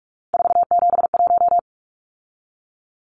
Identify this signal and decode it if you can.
Morse code, heard as a keyed tone, about 44 wpm, 700 Hz. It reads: 471
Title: Numbers - 3-Digit 45wpm